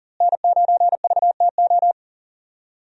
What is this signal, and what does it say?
Morse code, heard as a keyed tone, about 37 wpm, 700 Hz. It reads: N9VTO
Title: USA Call Signs - 5 Characters - Repeated 3x - 40wpm